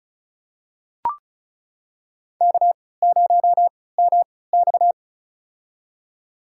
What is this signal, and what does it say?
Morse code, heard as a keyed tone, about 35 wpm, spaced as for 20 wpm, 690 Hz. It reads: K0MX
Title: USA Call Signs - 4 Characters - 3x Character Spacing - 35wpm